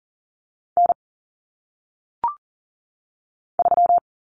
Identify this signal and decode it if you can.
Morse code, heard as a keyed tone, about 40 wpm, 700 Hz. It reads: N 3
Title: Common Cut Numbers - 1-Digit 40wpm